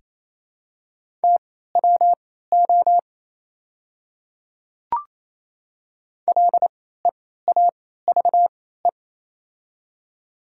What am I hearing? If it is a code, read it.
Morse code, heard as a keyed tone, about 28 wpm, spaced as for 16 wpm, 700 Hz. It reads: TWO LEAVE
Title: 3x Character Spacing - Top 500 Words - 28wpm